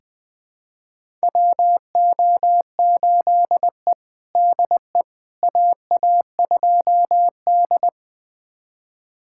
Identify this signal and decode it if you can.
Morse code, heard as a keyed tone, about 21 wpm, 700 Hz. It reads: WO8E DE AA2D
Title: USA Callsign DE Callsign - 4 Characters - 20wpm